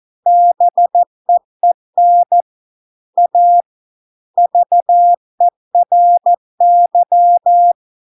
Morse code, heard as a keyed tone, about 14 wpm, 700 Hz. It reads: BEEN A VERY